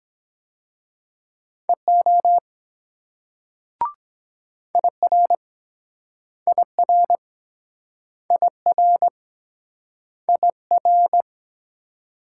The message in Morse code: EO IR IR IR IR

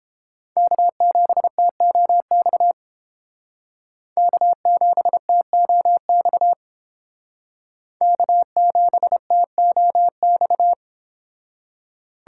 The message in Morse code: K7TOX K7TOX K7TOX

